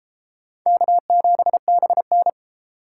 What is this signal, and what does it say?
Morse code, heard as a keyed tone, about 31 wpm, 700 Hz. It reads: K7BN